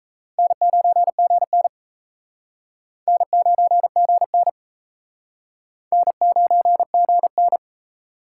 Morse code, heard as a keyed tone, about 34 wpm, 700 Hz. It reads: N9GN N9GN N9GN
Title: Speed Racing - USA 1x2 Call Signs 28wpm